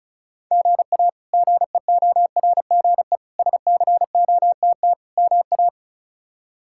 Morse code, heard as a keyed tone, about 34 wpm, 700 Hz. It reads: GA GEORGE SCOTT MA
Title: K1USN SST QSO Practice - 35wpm